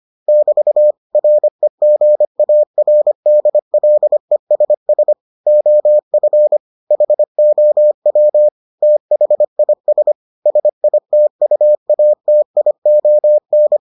Morse code, heard as a keyed tone, about 25 wpm, 600 Hz. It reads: <BT> REGARDLESS OF HOW THIS SITUATION